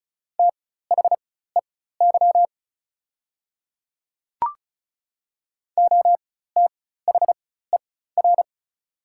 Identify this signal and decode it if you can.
Morse code, heard as a keyed tone, about 36 wpm, spaced as for 16 wpm, 690 Hz. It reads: THEY OTHER
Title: 4x Character Spacing - Top 300 Words - 35wpm